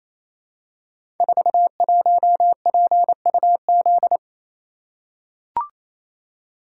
Morse code, heard as a keyed tone, about 27 wpm, 700 Hz. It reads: 41PUZ
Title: Code Groups - 5 Letter:Numbers - 28wpm